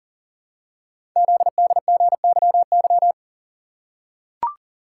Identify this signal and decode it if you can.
Morse code, heard as a keyed tone, about 42 wpm, 700 Hz. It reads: ZDGYY
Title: Code Groups - 5 Letters - 40wpm